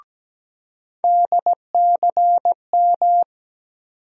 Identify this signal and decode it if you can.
Morse code, heard as a keyed tone, about 17 wpm, 700 Hz. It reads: DCM
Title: USA Call Sign Postfixes - 3 Letters 17wpm